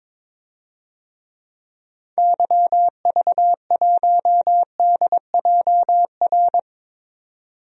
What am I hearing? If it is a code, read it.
Morse code, heard as a keyed tone, about 22 wpm, 700 Hz. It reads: YV1DJR
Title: DXCC-100 Call Signs - 6 Characters - 22wpm